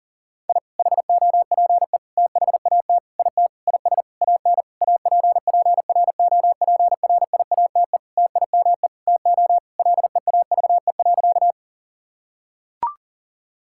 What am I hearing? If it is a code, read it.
Morse code, heard as a keyed tone, about 40 wpm, 700 Hz. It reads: I HOPE THAT IT IS AN APPROPRIATE TIME TO LEAVE.
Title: Sentences from Top 1000 Words - Review Part 5 - 40wpm